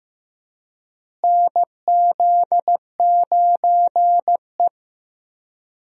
Morse code, heard as a keyed tone, about 15 wpm, 700 Hz. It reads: NZ9E
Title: USA Call Signs - 4 Characters - Repeated 3x - 15wpm